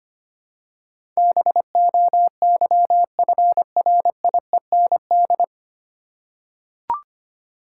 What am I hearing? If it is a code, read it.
Morse code, heard as a keyed tone, about 25 wpm, 700 Hz. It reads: BOYFRIEND
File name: practice / 25wpm / 9 Letter Words - First 3 Letters Given - 25wpm